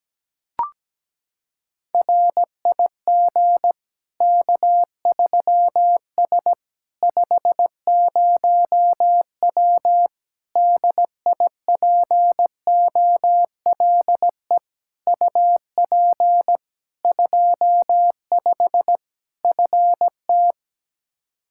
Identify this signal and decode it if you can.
Morse code, heard as a keyed tone, about 17 wpm, 700 Hz. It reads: RIG K3S 50W DIPOLE UP 25 FT
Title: Practice QSO - Exchange 2 - Info - 17wpm